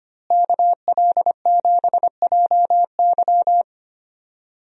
Morse code, heard as a keyed tone, about 25 wpm, 700 Hz. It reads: KL7JY
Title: DXCC-100 Call Signs - 5 Characters - Repeated 2x - 25wpm